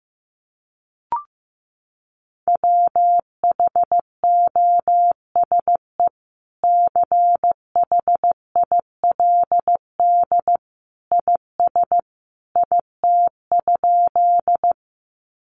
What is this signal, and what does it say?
Morse code, heard as a keyed tone, about 15 wpm, 700 Hz. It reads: WHOSE CHILD IS IT?
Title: Sentences from Top 1200 Words 15wpm